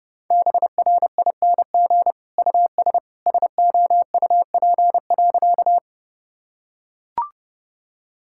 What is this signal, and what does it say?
Morse code, heard as a keyed tone, about 30 wpm, 700 Hz. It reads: BRING US SOUP.
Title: Sentences from Top 1000 Words - Review Part 6 - 30wpm